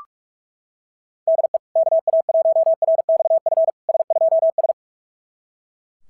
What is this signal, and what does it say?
Morse code, heard as a keyed tone, about 38 wpm, 660 Hz. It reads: DE KA1RXF S2S
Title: SOTA QSO Practice 45wpm